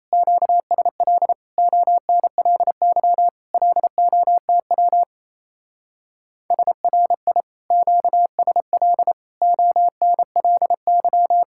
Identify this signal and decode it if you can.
Morse code, heard as a keyed tone, about 34 wpm, 700 Hz. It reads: QSL ONLY LOTW SRI QSL ONLY
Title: Speed-Racing Common QSO Phrases Encore - 28wpm